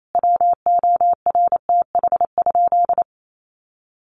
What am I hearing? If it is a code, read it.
Morse code, heard as a keyed tone, about 28 wpm, 700 Hz. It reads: WORTH?
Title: Sentences from Top 800 Words 28wpm